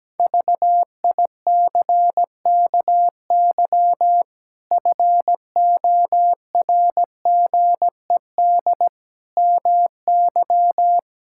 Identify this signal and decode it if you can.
Morse code, heard as a keyed tone, about 17 wpm, 700 Hz. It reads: VICKY FORGED MY
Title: Pangram Challenge v2 - 17wpm